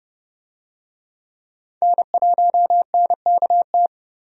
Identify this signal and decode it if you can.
Morse code, heard as a keyed tone, about 30 wpm, 700 Hz. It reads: N1NKT
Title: USA Call Signs - 5 Characters - 30wpm